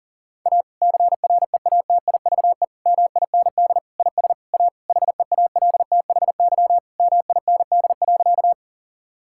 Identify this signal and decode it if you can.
Morse code, heard as a keyed tone, about 37 wpm, 700 Hz. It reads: A CREATIVE MIND IS A HEALTHY MIND.
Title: Sentences from Top 1000 Words - Review Part 5 - 40wpm